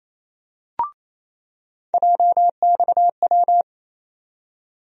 Morse code, heard as a keyed tone, about 27 wpm, 700 Hz. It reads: JXW
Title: USA Call Sign Postfixes - 3 Letters 28wpm